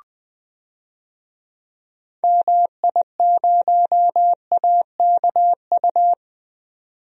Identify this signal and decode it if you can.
Morse code, heard as a keyed tone, about 20 wpm, 700 Hz. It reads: MI0AKU
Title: DXCC-100 Call Signs - 6 Characters - 20wpm